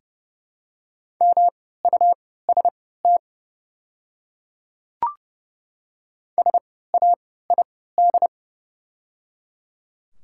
Morse code, heard as a keyed tone, about 30 wpm, spaced as for 17 wpm, 690 Hz. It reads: MUST SAID
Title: 3x Character Spacing - Top 300 Words - 30wpm